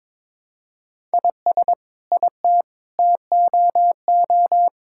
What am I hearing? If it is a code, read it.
Morse code, heard as a keyed tone, about 22 wpm, 700 Hz. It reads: IS IT TOO
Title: Sentences from Top 1000 Words - Review Part 6 - 22wpm